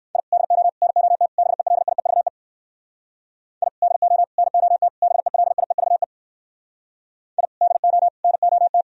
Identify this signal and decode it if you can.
Morse code, heard as a keyed tone, about 68 wpm, 700 Hz. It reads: I DO NOT BELIEVE I DO NOT BELIEVE I DO NOT